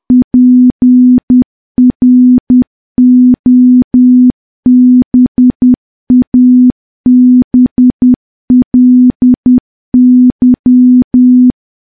Morse code, heard as a keyed tone, about 10 wpm, 250 Hz. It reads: PROBABLY
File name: morse / morse-training-10WPM